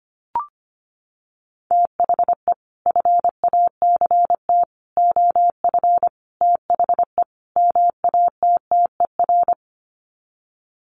Morse code, heard as a keyed tone, about 25 wpm, 700 Hz. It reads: THE FACT OF THE MATTER
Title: Sets of 5 Words 25wpm